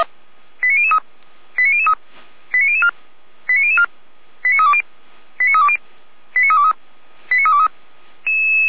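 Five-tone selective call (ZVEI), heard as quick runs of five tones, tones of about 70 ms. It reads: 89003 89003 89004 89004 89320 89320 89323 89323
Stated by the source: ZVEI five-tone selective call 89099, received by SDR, Feuerwehr Orlamünde fire brigade.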